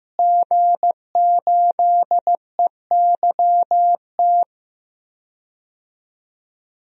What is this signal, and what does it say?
Morse code, heard as a keyed tone, about 15 wpm, 700 Hz. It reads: G8EYT
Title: DXCC-100 Call Signs - 5 Characters - Repeated 3x - 15wpm